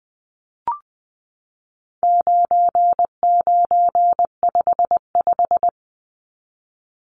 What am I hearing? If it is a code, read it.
Morse code, heard as a keyed tone, about 20 wpm, 700 Hz. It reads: 9955
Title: Numbers - 4-Digit 20wpm